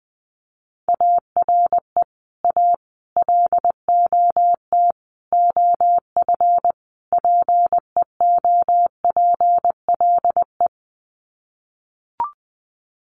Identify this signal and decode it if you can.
Morse code, heard as a keyed tone, about 20 wpm, 700 Hz. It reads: ARE A LOT OF PEOPLE
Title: Sets of 5 Words 20wpm